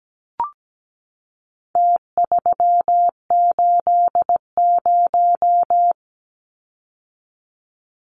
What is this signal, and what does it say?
Morse code, heard as a keyed tone, about 17 wpm, 700 Hz. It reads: T380
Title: Numbers - Contest Style - 4-Digit 17wpm